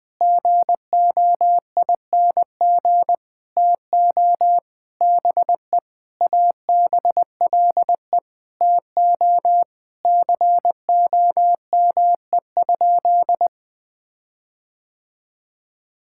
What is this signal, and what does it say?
Morse code, heard as a keyed tone, about 20 wpm, 700 Hz. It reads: GOING TO BE ABLE TO COME?
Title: Sentences with 5-Word Phrases - 20wpm